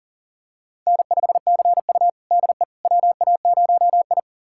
Morse code, heard as a keyed tone, about 37 wpm, 700 Hz. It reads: N5CU DE WA0I
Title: USA Callsign DE Callsign - 4 Characters - 40wpm